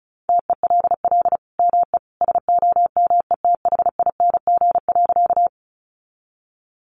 Morse code, heard as a keyed tone, about 34 wpm, 700 Hz. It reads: TELL ME SOMETHING.
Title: Sentences from Top 600 Words 35wpm